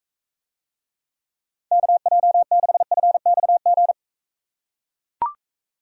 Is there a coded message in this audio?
Yes: Morse code from 1.7 to 3.9 s, heard as a keyed tone, about 43 wpm, 700 Hz. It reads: KJ6FXG